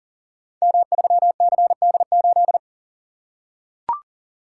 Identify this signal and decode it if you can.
Morse code, heard as a keyed tone, about 41 wpm, 690 Hz. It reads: M3CD8
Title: Code Groups - 5 Letter:Numbers - 40wpm